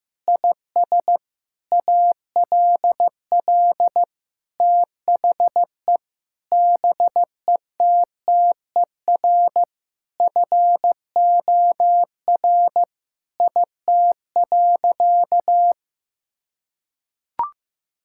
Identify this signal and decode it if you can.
Morse code, heard as a keyed tone, about 14 wpm, 700 Hz. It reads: IS ALL THE BETTER FOR IT.
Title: Sentences from Top 1000 Words - Review Part 5 - 15wpm